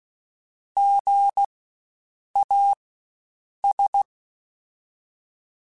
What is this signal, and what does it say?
Morse code, heard as a keyed tone, about 16 wpm, spaced as for 7 wpm, 800 Hz. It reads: GAS
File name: Spaced Alphas / Spaced Alphas mp3s / Gs sp @15